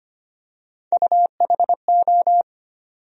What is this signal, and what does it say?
Morse code, heard as a keyed tone, about 25 wpm, 700 Hz. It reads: UHO
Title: USA Call Sign Postfixes - 3 Letters 25wpm